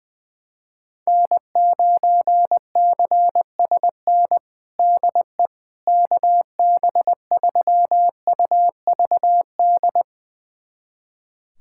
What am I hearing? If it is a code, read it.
Morse code, heard as a keyed tone, about 20 wpm, 700 Hz. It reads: N9CSN DE KB3UVD
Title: USA Callsign DE Callsign 20wpm